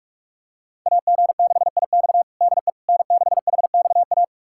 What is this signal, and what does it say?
Morse code, heard as a keyed tone, about 46 wpm, 700 Hz. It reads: AG6IX DE N6HXA
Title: USA Callsign DE Callsign - 5 Characters - 45wpm